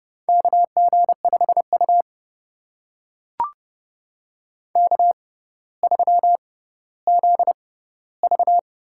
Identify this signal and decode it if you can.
Morse code, heard as a keyed tone, about 32 wpm, spaced as for 16 wpm, 700 Hz. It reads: KG5U K3ZV
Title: USA Call Signs - 6x Character Spacing - 30wpm